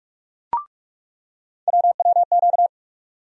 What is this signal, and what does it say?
Morse code, heard as a keyed tone, about 42 wpm, 700 Hz. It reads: WWQ
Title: USA Call Sign Postfixes - 3 Letters 45wpm